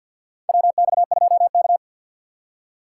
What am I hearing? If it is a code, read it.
Morse code, heard as a keyed tone, about 51 wpm, 700 Hz. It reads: WX2K
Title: USA Call Signs - 4 Characters - 6x Character Spacing - 50wpm